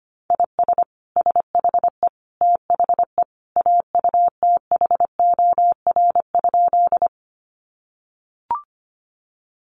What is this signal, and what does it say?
Morse code, heard as a keyed tone, about 25 wpm, 700 Hz. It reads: IS SHE THE AUTHOR?
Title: Sentences from Top 1200 Words 25wpm